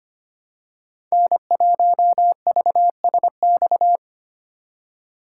Morse code, heard as a keyed tone, about 25 wpm, 700 Hz. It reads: N1VSX
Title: USA Call Signs - 5 Characters - 3x Character Spacing - 25wpm